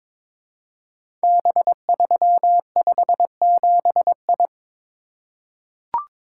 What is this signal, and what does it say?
Morse code, heard as a keyed tone, about 22 wpm, 700 Hz. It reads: B357I
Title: Code Groups - 5 Letter:Numbers - 22wpm